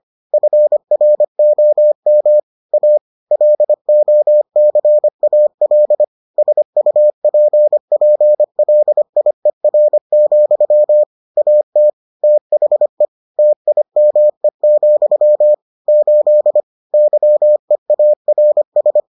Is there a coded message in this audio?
Yes: Morse code from 0.3 to 19.0 s, heard as a keyed tone, about 25 wpm, 600 Hz. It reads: FROM A LOCAL SUPPLIER, AT THE TIME, 8 YEARS